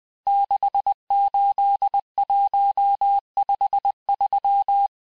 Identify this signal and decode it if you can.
Morse code, heard as a keyed tone, about 20 wpm, 800 Hz. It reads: 68153
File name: cours-1584-teste-chiffre-vitesse-19mpm